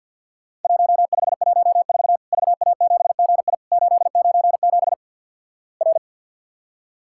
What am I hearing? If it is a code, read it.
Morse code, heard as a keyed tone, about 50 wpm, 700 Hz. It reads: 1514 VA7GI 897 R
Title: Worked All Europe DX Contest - QTC Exchange - 50wpm